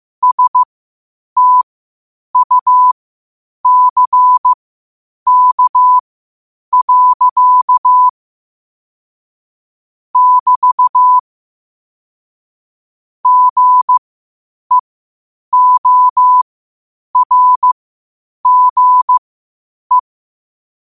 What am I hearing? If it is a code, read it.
Morse code, heard as a keyed tone, about 15 wpm, spaced as for 8 wpm, 1000 Hz. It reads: STUCK. <BT> GEORGE
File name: qotdfiles / 8wpm